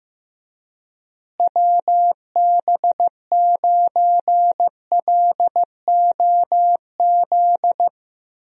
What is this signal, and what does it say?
Morse code, heard as a keyed tone, about 16 wpm, 700 Hz. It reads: WB9LOZ
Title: USA Call Signs - 6 Characters - Repeated 2x - 15wpm